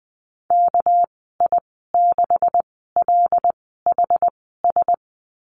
Morse code, heard as a keyed tone, about 19 wpm, spaced as for 14 wpm, 700 Hz. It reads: KI6LHS
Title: 207 Lesson 33g - Letter j - Call Signs 20wpm